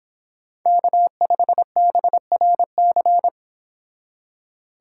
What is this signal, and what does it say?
Morse code, heard as a keyed tone, about 26 wpm, 700 Hz. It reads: K5BRC